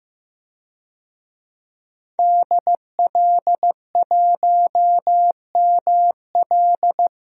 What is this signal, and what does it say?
Morse code, heard as a keyed tone, about 15 wpm, 700 Hz. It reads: DL1ML
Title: DXCC-100 Call Signs - 5 Characters - 15wpm